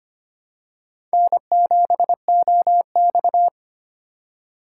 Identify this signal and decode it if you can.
Morse code, heard as a keyed tone, about 25 wpm, 700 Hz. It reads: N7OX
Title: USA Call Signs - 4x Character Spacing - 25wpm